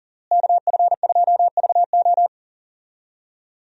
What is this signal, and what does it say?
Morse code, heard as a keyed tone, about 40 wpm, 700 Hz. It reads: KF2VO
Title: USA Call Signs - 5 Characters - Repeated 3x - 40wpm